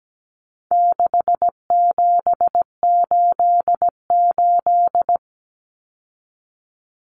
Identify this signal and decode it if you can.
Morse code, heard as a keyed tone, about 17 wpm, 700 Hz. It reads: 6788
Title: Numbers - 4-Digit 17wpm